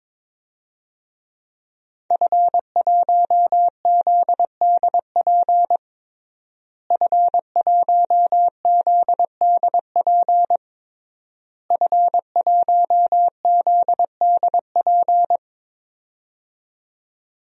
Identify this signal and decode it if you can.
Morse code, heard as a keyed tone, about 22 wpm, 700 Hz. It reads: F1ZDP F1ZDP F1ZDP
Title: DXCC-100 Call Signs - 5 Characters - Repeated 3x - 22wpm